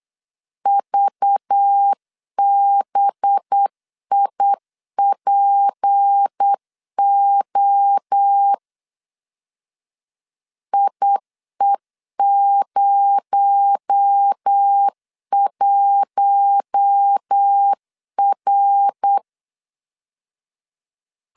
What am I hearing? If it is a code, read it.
Morse code, heard as a keyed tone, about 8 wpm, 800 Hz. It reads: VBIPO IE01R
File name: LEZIONI / 6 PKX0 / 6 PKX0 8wpm 3341